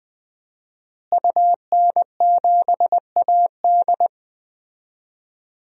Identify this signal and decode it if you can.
Morse code, heard as a keyed tone, about 20 wpm, 700 Hz. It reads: UN7AD
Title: DXCC-100 Call Signs - 5 Characters - Repeated 3x - 20wpm